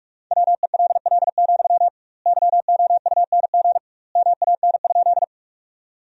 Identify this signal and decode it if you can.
Morse code, heard as a keyed tone, about 46 wpm, 700 Hz. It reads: WELL, YOUNG MAN?
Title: 2x Word-Spacing - Sentences from Top 400 Words - 45wpm